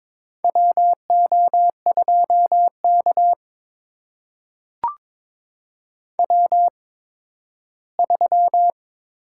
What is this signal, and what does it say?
Morse code, heard as a keyed tone, about 21 wpm, spaced as for 13 wpm, 700 Hz. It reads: WO2K W3
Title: USA Call Signs - 4 Characters - 8x Character Spacing - 22wpm